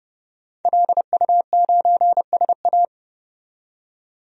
Morse code, heard as a keyed tone, about 30 wpm, 700 Hz. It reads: LU9SA
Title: DXCC-100 Call Signs - 5 Characters - 30wpm